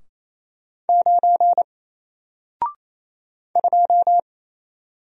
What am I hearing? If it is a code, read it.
Morse code, heard as a keyed tone, about 27 wpm, 700 Hz. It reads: 9 2
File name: Single-Letter-Number-28wpm